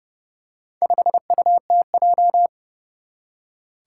Morse code, heard as a keyed tone, about 30 wpm, 700 Hz. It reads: 5UTJ
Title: Four Challenging Letter-Number Combinations 30wpm